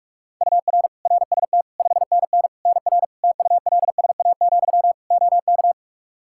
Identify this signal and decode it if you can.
Morse code, heard as a keyed tone, about 46 wpm, 700 Hz. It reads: UR RST 5NN NR TULSA, OK
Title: Practice QSO - Exchange 1 - Beginning - 45wpm